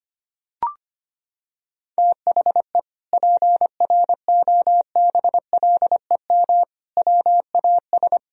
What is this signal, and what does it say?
Morse code, heard as a keyed tone, about 25 wpm, 700 Hz. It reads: THE PROBLEM WAS
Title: Sentences from Top 1000 Words - Review Part 2 - 25wpm